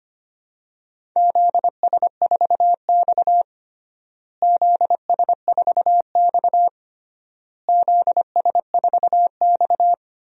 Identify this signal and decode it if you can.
Morse code, heard as a keyed tone, about 25 wpm, 700 Hz. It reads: ZS4X ZS4X ZS4X
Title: DXCC-100 Call Signs - 4 Characters - Repeated 3x - 25wpm